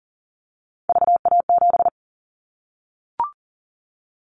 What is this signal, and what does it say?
Morse code, heard as a keyed tone, about 40 wpm, 700 Hz. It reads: VA7
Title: Callsign Prefixes for Top 50 DXCC - 40wpm